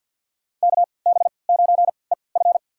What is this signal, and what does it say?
Morse code, heard as a keyed tone, about 53 wpm, spaced as for 29 wpm, 700 Hz. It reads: KB9EF
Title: USA Call Signs - 5 Characters - 3x Character Spacing - 50wpm